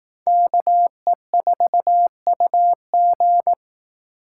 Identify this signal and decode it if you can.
Morse code, heard as a keyed tone, about 18 wpm, 700 Hz. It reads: KE4UG
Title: Speed Racing - USA 2x2 Call Signs 15wpm